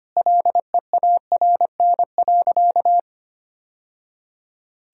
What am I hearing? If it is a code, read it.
Morse code, heard as a keyed tone, about 25 wpm, 700 Hz. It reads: LEARN.
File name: practice / 25wpm / Sentences with 2-Word Phrases - 25wpm